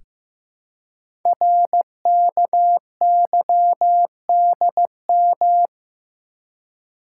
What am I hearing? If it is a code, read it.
Morse code, heard as a keyed tone, about 15 wpm, 700 Hz. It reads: RKYDM
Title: Code Groups - 5 Letters - 15wpm